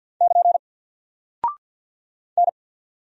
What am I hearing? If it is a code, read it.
Morse code, heard as a keyed tone, about 48 wpm, 700 Hz. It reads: <KN> N
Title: Single Character (Letter:Number:Punctuation:Prosign) 50wpm